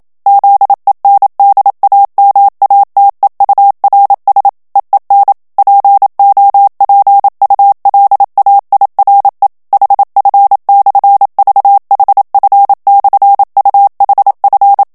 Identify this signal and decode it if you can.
Morse code, heard as a keyed tone, about 27 wpm, 800 Hz. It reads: ZENDAMATEURS EEN POPULAIRE HF/VHF/UHF
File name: HOLLANDAIS texte clair 1 vitesse 26 mots minute